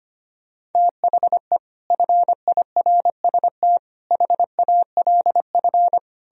Morse code, heard as a keyed tone, about 32 wpm, 700 Hz. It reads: THE FIRST HALF